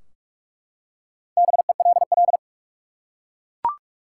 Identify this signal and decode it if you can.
Morse code, heard as a keyed tone, about 41 wpm, 700 Hz. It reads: BELL
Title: Four Letter Words - 45wpm